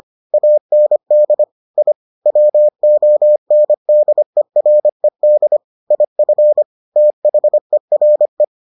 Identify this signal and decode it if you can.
Morse code, heard as a keyed tone, about 25 wpm, 600 Hz. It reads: AND I WONDERED IF THERE